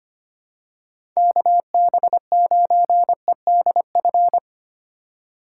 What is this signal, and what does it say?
Morse code, heard as a keyed tone, about 25 wpm, 700 Hz. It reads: KB9EDF
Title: USA Call Signs - 4x Character Spacing - 25wpm